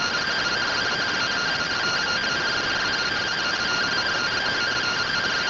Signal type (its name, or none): two-tone teleprinter signal